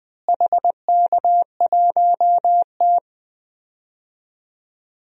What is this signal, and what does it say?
Morse code, heard as a keyed tone, about 20 wpm, 700 Hz. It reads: HK1T
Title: DXCC-100 Call Signs - 4 Characters - Repeated 3x - 20wpm